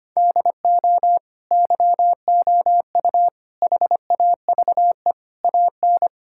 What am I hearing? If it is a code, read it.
Morse code, heard as a keyed tone, about 25 wpm, 700 Hz. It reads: DO YOU HAVE AN
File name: Sets of 9 Words - 25wpm